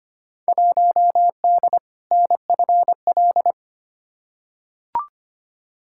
Morse code, heard as a keyed tone, about 25 wpm, 700 Hz. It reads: 1D NFL
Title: Field Day Exchanges - 25wpm